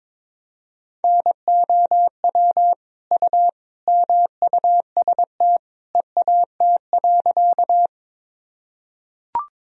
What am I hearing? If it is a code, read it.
Morse code, heard as a keyed tone, about 22 wpm, 700 Hz. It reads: NOW U MUST EAT.
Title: Sentences from Top 300 Words and Abbreviations 22wpm